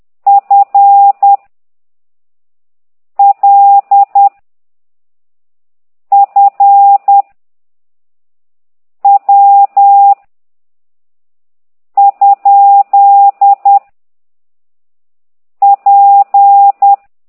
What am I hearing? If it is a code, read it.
Morse code, heard as a keyed tone, about 10 wpm, 800 Hz. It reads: F L F W ? P
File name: ct2idw / exerc8